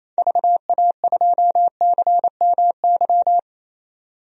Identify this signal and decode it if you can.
Morse code, heard as a keyed tone, about 28 wpm, 700 Hz. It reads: VA2CMY
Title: DXCC-100 Call Signs - 6 Characters - 28wpm